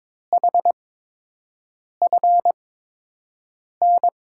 Morse code, heard as a keyed tone, about 22 wpm, spaced as for 6 wpm, 700 Hz. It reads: HFN
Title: USA Call Signs - 5 Characters - 8x Character Spacing - 22wpm